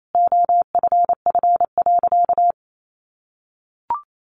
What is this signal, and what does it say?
Morse code, heard as a keyed tone, about 28 wpm, 700 Hz. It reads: OFF.
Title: Sentences from Top 1300 Words 28wpm